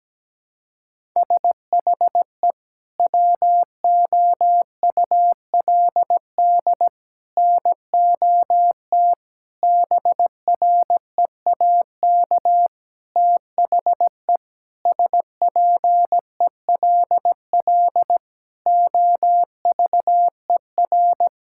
Morse code, heard as a keyed tone, about 17 wpm, 700 Hz. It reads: SHE WOULD NOT BREAK THE SPELL OVER
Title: Sentences from Top 1000 Words - Review Part 6 - 17wpm